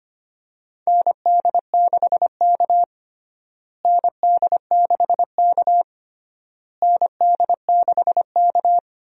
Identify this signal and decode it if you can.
Morse code, heard as a keyed tone, about 25 wpm, 700 Hz. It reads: ND6K ND6K ND6K
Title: USA Call Signs - 4 Characters - Repeated 3x - 25wpm